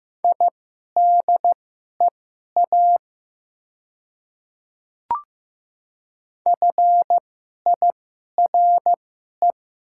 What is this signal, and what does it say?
Morse code, heard as a keyed tone, about 15 wpm, spaced as for 11 wpm, 700 Hz. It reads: IDEA FIRE
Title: 2x Character Spacing - Top 500 Words - 15wpm